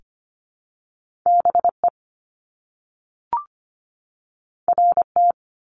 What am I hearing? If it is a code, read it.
Morse code, heard as a keyed tone, about 25 wpm, 700 Hz. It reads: BE RT
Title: Top Two Letter Combinations in Words 25wpm